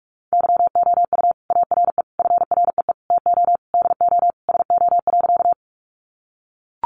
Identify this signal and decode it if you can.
Morse code, heard as a keyed tone, about 44 wpm, 700 Hz. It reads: YOU ARE FREE TO DO SO.
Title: Sentences from Top 500 Words 45wpm